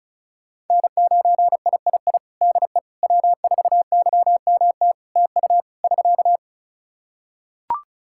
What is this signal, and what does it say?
Morse code, heard as a keyed tone, about 36 wpm, 700 Hz. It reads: N9III DE W4YMT TU <SK>
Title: Practice QSO - Exchange 4 - Ending - 35wpm